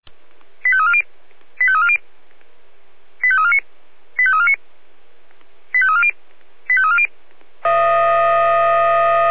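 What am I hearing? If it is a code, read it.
Five-tone selective call (ZVEI), heard as quick runs of five tones, tones of about 70 ms. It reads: 85340 85340 85349 85349 85340 85340
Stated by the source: Five-tone ZVEI selective call received by SDR.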